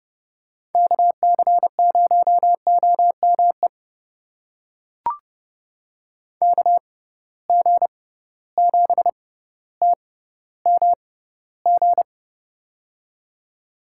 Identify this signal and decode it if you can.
Morse code, heard as a keyed tone, about 31 wpm, spaced as for 15 wpm, 700 Hz. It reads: KC0OME KG7TMG
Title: USA Call Signs - 6 Characters - 6x Character Spacing - 30wpm